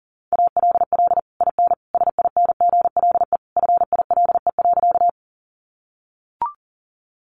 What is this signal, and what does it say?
Morse code, heard as a keyed tone, about 40 wpm, 700 Hz. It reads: ALL IN SINGLE FILE.
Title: Sentences from Top 1200 Words 40wpm